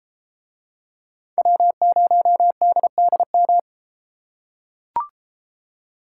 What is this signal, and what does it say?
Morse code, heard as a keyed tone, about 33 wpm, 700 Hz. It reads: W0DDM